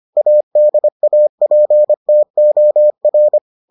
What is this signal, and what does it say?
Morse code, heard as a keyed tone, about 25 wpm, 600 Hz. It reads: ADAPTOR